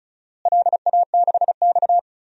Morse code, heard as a keyed tone, about 34 wpm, 700 Hz. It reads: LA6X
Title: DXCC-100 Call Signs - 4 Characters - Repeated 2x - 35wpm